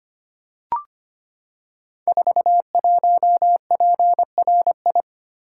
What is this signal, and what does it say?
Morse code, heard as a keyed tone, about 25 wpm, 700 Hz. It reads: 41PRI
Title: Code Groups - 5 Letter:Numbers - 25wpm